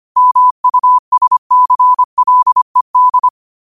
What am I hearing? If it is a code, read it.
Morse code, heard as a keyed tone, about 25 wpm, 1000 Hz. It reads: MUSCLED
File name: qotdfiles / 25wpm